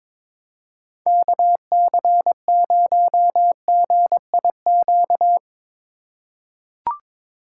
Morse code, heard as a keyed tone, about 22 wpm, 700 Hz. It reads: KC0GIQ
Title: USA Call Signs - 6 Characters - 2x Character Spacing - 22wpm